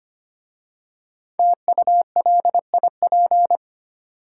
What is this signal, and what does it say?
Morse code, heard as a keyed tone, about 25 wpm, 690 Hz. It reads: TULIP
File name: practice / 25wpm / Easy Words with Hints - 25wpm